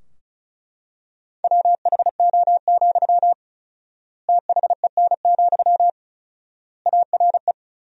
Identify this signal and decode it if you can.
Morse code, heard as a keyed tone, about 36 wpm, 700 Hz. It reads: WHO, THEN, ARE
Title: 4x Word-Spacing - Sentences from Top 100 Words - 35wpm